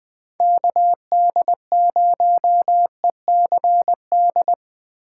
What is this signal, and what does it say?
Morse code, heard as a keyed tone, about 23 wpm, 700 Hz. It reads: KD0ECD
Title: Speed Racing - USA 2x3 Call Signs 20wpm